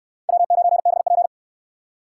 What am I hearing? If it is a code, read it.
Morse code, heard as a keyed tone, about 62 wpm, 700 Hz. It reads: W9BP